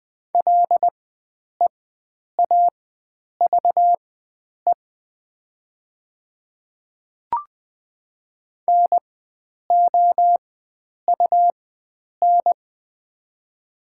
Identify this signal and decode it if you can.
Morse code, heard as a keyed tone, about 20 wpm, spaced as for 9 wpm, 700 Hz. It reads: LEAVE NOUN
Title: 4x Character Spacing - Top 500 Words - 20wpm